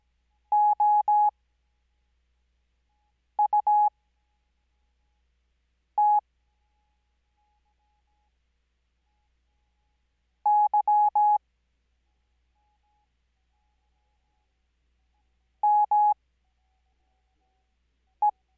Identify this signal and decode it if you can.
Morse code, heard as a keyed tone, about 17 wpm, spaced as for 4 wpm, 840 Hz. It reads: OUT Y ME
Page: Farnsworth Spacing: 15WPM Character speed Spaced at 5WPM.